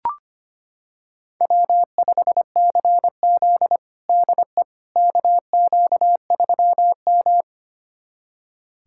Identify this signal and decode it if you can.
Morse code, heard as a keyed tone, about 25 wpm, 700 Hz. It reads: W5CZ DE KQ3M
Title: USA Callsign DE Callsign - 4 Characters - 25wpm